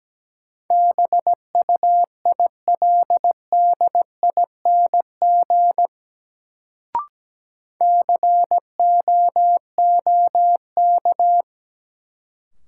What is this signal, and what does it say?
Morse code, heard as a keyed tone, about 17 wpm, 700 Hz. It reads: BUILDING COOK
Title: Sentences from Top 700 Words - New Word Drill - 17wpm